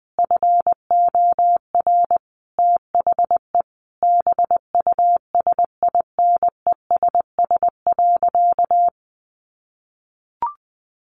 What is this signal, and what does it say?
Morse code, heard as a keyed tone, about 20 wpm, 700 Hz. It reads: FOR THE BUSINESS.
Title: Sentences from Top 1400 Words 20wpm